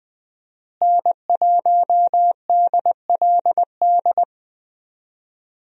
Morse code, heard as a keyed tone, about 20 wpm, 700 Hz. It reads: N1DLD
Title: USA Call Signs - 5 Characters - Repeated 2x - 20wpm